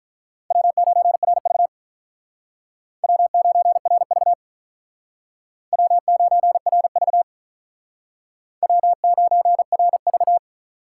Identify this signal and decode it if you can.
Morse code, heard as a keyed tone, about 44 wpm, 700 Hz. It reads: W9RV W9RV W9RV W9RV